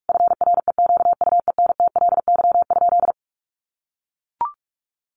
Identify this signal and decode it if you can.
Morse code, heard as a keyed tone, about 44 wpm, 700 Hz. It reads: FREQUENTLY?
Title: Sentences from Top 1300 Words 45wpm